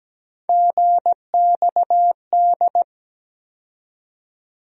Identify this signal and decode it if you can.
Morse code, heard as a keyed tone, about 17 wpm, 700 Hz. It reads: GXD
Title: USA Call Sign Postfixes - 3 Letters 17wpm